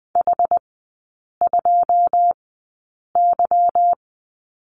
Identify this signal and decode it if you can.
Morse code, heard as a keyed tone, about 20 wpm, spaced as for 8 wpm, 700 Hz. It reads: H2Y
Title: 152 Lesson 25e - Letter p - Three Characters 20wpm